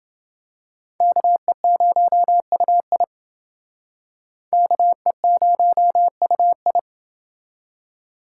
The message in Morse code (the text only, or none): KE0UI KE0UI